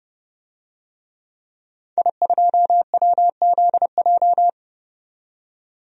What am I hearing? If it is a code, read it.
Morse code, heard as a keyed tone, about 30 wpm, 700 Hz. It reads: I2WZJ
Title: DXCC-100 Call Signs - 5 Characters - 30wpm